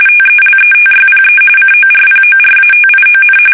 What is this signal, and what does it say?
RTTY, about 45.45 baud. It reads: TELETYPE CORPORATION
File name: fsk850 / teletype-corp